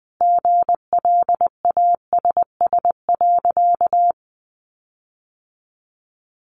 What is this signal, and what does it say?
Morse code, heard as a keyed tone, about 20 wpm, 700 Hz. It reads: GLASS.
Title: Sentences from Top 1300 Words 20wpm